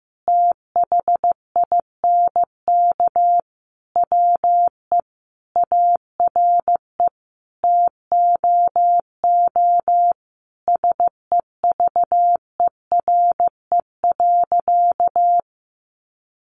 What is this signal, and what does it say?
Morse code, heard as a keyed tone, about 15 wpm, 700 Hz. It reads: THINK WE ARE TOO SEVERE.
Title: Sentences from Top 1400 Words 15wpm